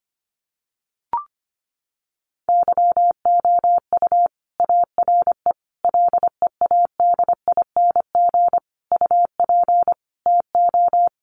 Morse code, heard as a keyed tone, about 25 wpm, 700 Hz. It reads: YOU ARE LEADING UP TO
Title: Sentences from Top 1100 Words 25wpm